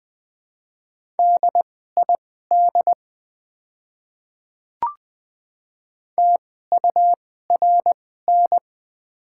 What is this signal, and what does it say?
Morse code, heard as a keyed tone, about 20 wpm, spaced as for 15 wpm, 690 Hz. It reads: DID TURN
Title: 2x Character Spacing - Top 200 Words - 20wpm